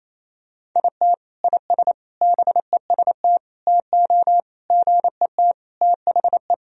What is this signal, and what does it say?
Morse code, heard as a keyed tone, about 28 wpm, 700 Hz. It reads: IT IS BEST TO GET THE
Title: Sentences from Top 1000 Words - Review Part 1 - 28wpm